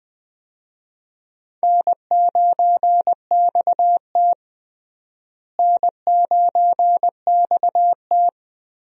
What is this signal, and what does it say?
Morse code, heard as a keyed tone, about 21 wpm, 700 Hz. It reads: N9XT N9XT
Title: USA Call Signs - 4 Characters - Repeated 3x - 20wpm